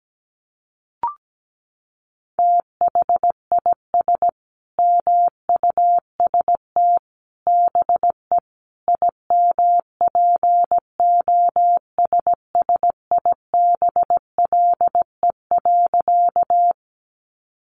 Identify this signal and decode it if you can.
Morse code, heard as a keyed tone, about 17 wpm, 700 Hz. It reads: THIS MUST BE IMPOSSIBLE.
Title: Sentences from Top 1100 Words 17wpm